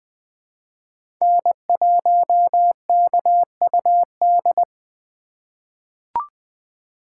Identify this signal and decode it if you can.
Morse code, heard as a keyed tone, about 20 wpm, 700 Hz. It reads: N1KUD
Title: USA Call Signs - 5 Characters - 2x Character Spacing - 20wpm